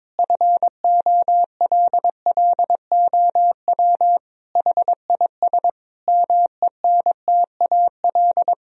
Morse code, heard as a keyed tone, about 21 wpm, 700 Hz. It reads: FOLLOW HIS MENTAL
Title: Sentences from Top 1000 Words - Review Part 4 - 22wpm